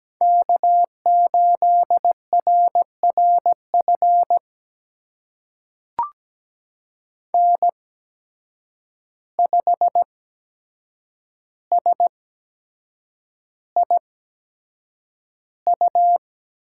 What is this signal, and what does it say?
Morse code, heard as a keyed tone, about 17 wpm, spaced as for 7 wpm, 700 Hz. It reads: K8RRF N5SIU
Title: USA Call Signs - 8x Character Spacing - 17wpm